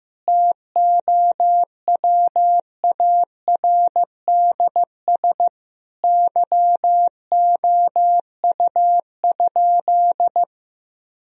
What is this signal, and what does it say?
Morse code, heard as a keyed tone, about 15 wpm, 700 Hz. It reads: TOWARDS YOU?
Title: Sentences from Top 1000 Words - Review Part 2 - 15wpm